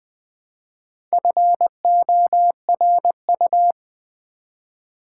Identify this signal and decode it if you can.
Morse code, heard as a keyed tone, about 21 wpm, 700 Hz. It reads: FORU
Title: Build A Word - 1 Letter At A Time - 5-Letter Words - 20wpm